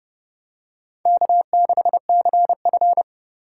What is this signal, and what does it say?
Morse code, heard as a keyed tone, about 28 wpm, 700 Hz. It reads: K6CF